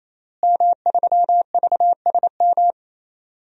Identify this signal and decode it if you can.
Morse code, heard as a keyed tone, about 27 wpm, 700 Hz. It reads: M3VSM
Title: DXCC-100 Call Signs - 5 Characters - Repeated 3x - 28wpm